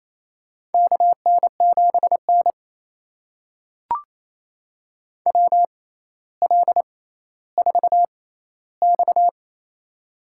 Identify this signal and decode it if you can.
Morse code, heard as a keyed tone, about 28 wpm, spaced as for 14 wpm, 700 Hz. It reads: KN7N WL4X
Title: USA Call Signs - 4 Characters - 6x Character Spacing - 28wpm